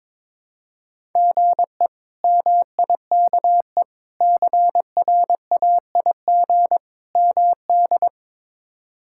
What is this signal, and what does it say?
Morse code, heard as a keyed tone, about 22 wpm, 700 Hz. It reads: GE MIKE CRAIG MD
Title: K1USN SST QSO Practice - 22wpm